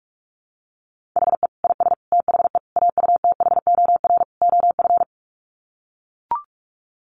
Morse code, heard as a keyed tone, about 44 wpm, 700 Hz. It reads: HE IS THE AUTHOR OF
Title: Sets of 5 Words Encore2 45wpm